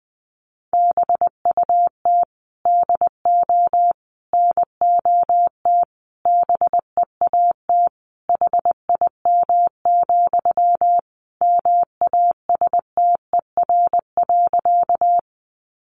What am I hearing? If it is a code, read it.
Morse code, heard as a keyed tone, about 20 wpm, 700 Hz. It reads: BUT DO NOT BEAT HIM, MASTER.
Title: Sentences from Top 1300 Words 20wpm